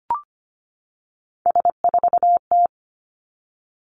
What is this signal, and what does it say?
Morse code, heard as a keyed tone, about 25 wpm, 700 Hz. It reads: S4T
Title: Three Challenging Letter-Number Combinations 25wpm